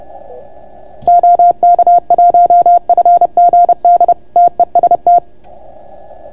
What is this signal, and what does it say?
Morse code, heard as a keyed tone, about 30 wpm, 690 Hz. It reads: OK1FGD TEST